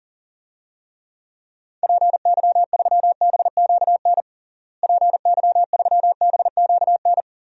Morse code, heard as a keyed tone, about 40 wpm, 700 Hz. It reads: PY3BQN PY3BQN
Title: DXCC-100 Call Signs - 6 Characters - Repeated 2x - 40wpm